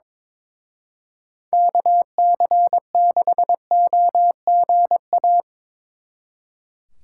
Morse code, heard as a keyed tone, about 24 wpm, 700 Hz. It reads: KC6OGA
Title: Speed Racing - USA 2x3 Call Signs 22wpm